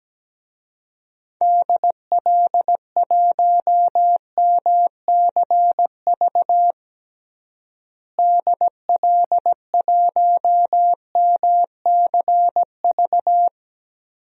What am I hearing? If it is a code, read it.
Morse code, heard as a keyed tone, about 17 wpm, 700 Hz. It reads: DL1MCV DL1MCV
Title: DXCC-100 Call Signs - 6 Characters - Repeated 3x - 17wpm